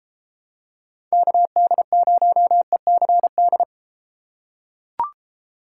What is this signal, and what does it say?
Morse code, heard as a keyed tone, about 31 wpm, 700 Hz. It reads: KD0ECD